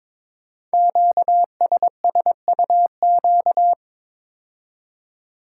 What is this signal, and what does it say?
Morse code, heard as a keyed tone, about 22 wpm, 700 Hz. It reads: QSSUQ
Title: Code Groups - 5 Letters - 22wpm